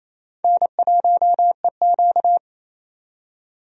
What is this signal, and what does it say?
Morse code, heard as a keyed tone, about 28 wpm, 700 Hz. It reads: N1EQ
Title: USA Call Signs - 4 Characters - 28wpm